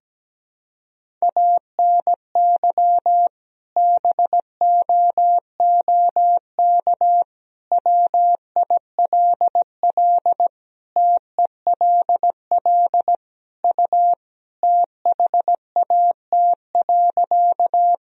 Morse code, heard as a keyed tone, about 17 wpm, 700 Hz. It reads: ANY BOOK WILL TELL U THAT.
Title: Sentences from Top 300 Words and Abbreviations 17wpm